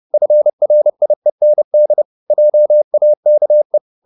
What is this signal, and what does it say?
Morse code, heard as a keyed tone, about 30 wpm, 600 Hz. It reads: FRIEND JAKE